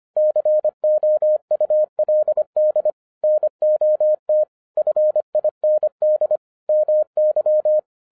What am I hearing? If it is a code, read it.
Morse code, heard as a keyed tone, about 25 wpm, 610 Hz. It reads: COULD NOT FIND MY